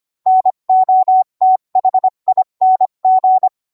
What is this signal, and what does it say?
Morse code, heard as a keyed tone, about 25 wpm, 750 Hz. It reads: NOTHING